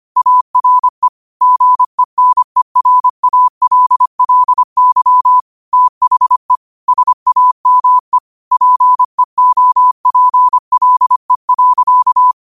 Morse code, heard as a keyed tone, about 25 wpm, 1000 Hz. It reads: ARE GENERALLY THE SAME PEOPLE.
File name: qotdfiles / 25wpm